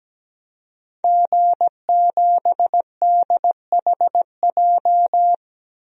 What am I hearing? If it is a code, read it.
Morse code, heard as a keyed tone, about 17 wpm, 700 Hz. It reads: G7DHJ
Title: DXCC-100 Call Signs - 5 Characters - Repeated 3x - 17wpm